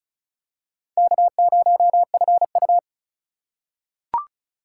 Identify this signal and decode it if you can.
Morse code, heard as a keyed tone, about 36 wpm, 700 Hz. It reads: K0FU
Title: USA Call Signs - 4 Characters - 4x Character Spacing - 35wpm